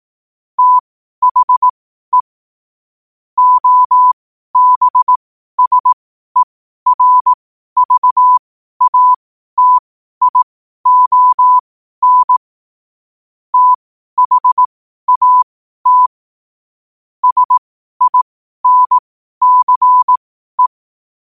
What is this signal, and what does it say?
Morse code, heard as a keyed tone, about 18 wpm, spaced as for 12 wpm, 1000 Hz. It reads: THE OBSERVATION THAT SINCE
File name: qotdfiles / 12wpm